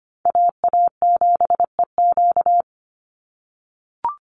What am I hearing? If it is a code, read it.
Morse code, heard as a keyed tone, about 25 wpm, 700 Hz. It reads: AA7EQ
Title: NA SOTA Chaser Callsigns 25wpm